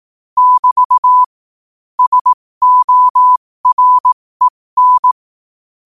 Morse code, heard as a keyed tone, about 18 wpm, spaced as for 15 wpm, 1000 Hz. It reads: <BT> SOREN
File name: qotdfiles / 15wpm